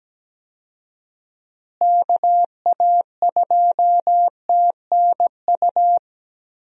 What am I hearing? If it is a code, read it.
Morse code, heard as a keyed tone, about 17 wpm, 700 Hz. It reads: KA2TNU
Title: USA Call Signs - 6 Characters - Repeated 2x - 17wpm